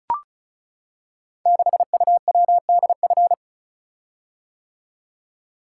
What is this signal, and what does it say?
Morse code, heard as a keyed tone, about 36 wpm, 700 Hz. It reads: 6UWDF
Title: Code Groups - 5 Letter:Numbers - 35wpm